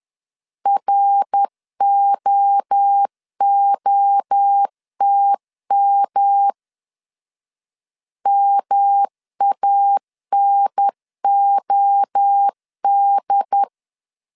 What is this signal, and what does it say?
Morse code, heard as a keyed tone, about 10 wpm, 800 Hz. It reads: ROOTM MANOD
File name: CW / DET / 1 3 1500C S B 10wpm 3335